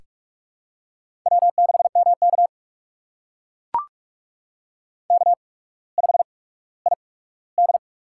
Morse code, heard as a keyed tone, about 44 wpm, spaced as for 19 wpm, 700 Hz. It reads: W6MK K5ID
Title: USA Call Signs - 4 Characters - 8x Character Spacing - 45wpm